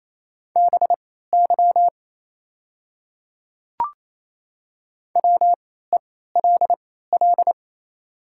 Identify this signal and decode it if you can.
Morse code, heard as a keyed tone, about 28 wpm, spaced as for 16 wpm, 700 Hz. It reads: BY WELL
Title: 3x Character Spacing - Top 300 Words - 28wpm